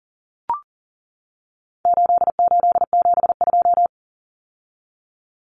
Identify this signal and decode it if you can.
Morse code, heard as a keyed tone, about 40 wpm, 700 Hz. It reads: 8872
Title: Numbers - 4-Digit 40wpm